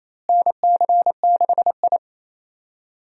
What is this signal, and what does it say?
Morse code, heard as a keyed tone, about 28 wpm, 700 Hz. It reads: NC6I
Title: USA Call Signs - 4 Characters - Repeated 2x - 28wpm